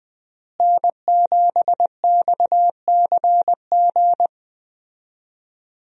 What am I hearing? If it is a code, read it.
Morse code, heard as a keyed tone, about 20 wpm, 700 Hz. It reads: N7XCG
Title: USA Call Signs - 5 Characters - Repeated 3x - 20wpm